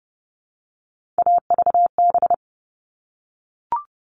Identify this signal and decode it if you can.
Morse code, heard as a keyed tone, about 30 wpm, 700 Hz. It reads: AVB
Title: All Cut Numbers - 3-Digit 30wpm